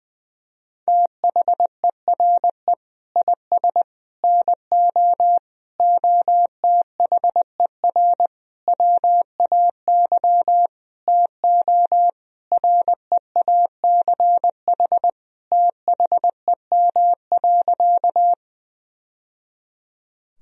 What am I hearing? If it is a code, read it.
Morse code, heard as a keyed tone, about 21 wpm, 700 Hz. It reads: THERE IS NO OTHER WAY TO REACH THEM.
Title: Sentences from Top 1000 Words - Review Part 2 - 20wpm